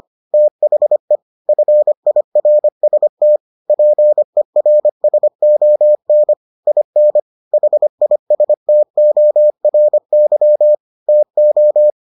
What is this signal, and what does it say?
Morse code, heard as a keyed tone, about 25 wpm, 600 Hz. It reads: THE FIRST PERSON IN HISTORY TO